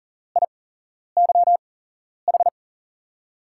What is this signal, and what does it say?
Morse code, heard as a keyed tone, about 40 wpm, spaced as for 11 wpm, 700 Hz. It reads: IYH
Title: USA Call Signs - 5 Characters - 8x Character Spacing - 40wpm